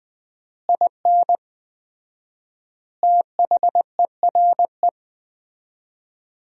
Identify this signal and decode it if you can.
Morse code, heard as a keyed tone, about 21 wpm, 700 Hz. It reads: IN THERE
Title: 4x Word-Spacing - Sentences from Top 500 Words - 20wpm